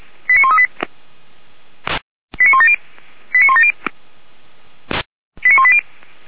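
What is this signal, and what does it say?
Five-tone selective call (ZVEI), heard as quick runs of five tones, tones of about 70 ms. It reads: 89138 89170 89170 89180
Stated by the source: ZVEI five-tone selective call received by SDR, code 89135, recorded by feuerwehr-orlamuende.org fire brigade.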